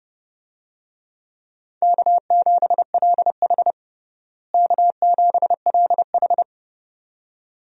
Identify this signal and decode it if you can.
Morse code, heard as a keyed tone, about 30 wpm, 700 Hz. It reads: K7LH K7LH
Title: USA Call Signs - 4 Characters - Repeated 2x - 30wpm